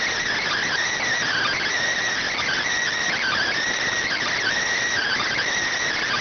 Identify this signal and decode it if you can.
Emergency Alert System (SAME) tones (header bursts), heard as harsh buzzing bursts. It is heard from 0.6 to 5.3 s.